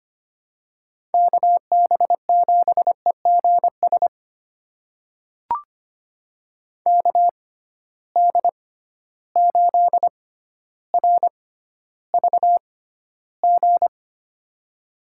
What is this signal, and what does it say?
Morse code, heard as a keyed tone, about 25 wpm, spaced as for 13 wpm, 700 Hz. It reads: KB7EGS KD8RVG
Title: USA Call Signs - 6 Characters - 6x Character Spacing - 25wpm